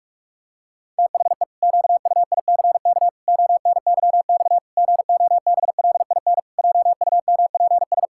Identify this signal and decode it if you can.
Morse code, heard as a keyed tone, about 43 wpm, 700 Hz. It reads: THE QUICK ONYX GOBLIN JUMPS
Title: Pangram Challenge v2 - 45wpm